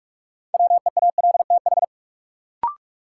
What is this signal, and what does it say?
Morse code, heard as a keyed tone, about 46 wpm, 700 Hz. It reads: WEALTH
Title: Sentences from Top 1200 Words - New Word Drill Alt - 45wpm